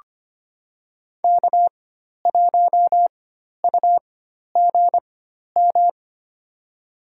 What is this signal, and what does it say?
Morse code, heard as a keyed tone, about 25 wpm, spaced as for 12 wpm, 700 Hz. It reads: K1UGM
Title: USA Call Signs - 5 Characters - 4x Character Spacing - 25wpm